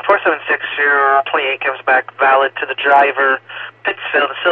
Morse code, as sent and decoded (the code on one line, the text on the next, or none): none